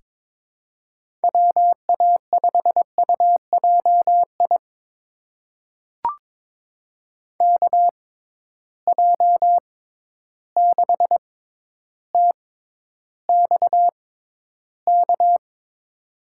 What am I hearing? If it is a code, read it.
Morse code, heard as a keyed tone, about 22 wpm, spaced as for 11 wpm, 700 Hz. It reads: WA5UJI KJ6TXK
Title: USA Call Signs - 6 Characters - 6x Character Spacing - 22wpm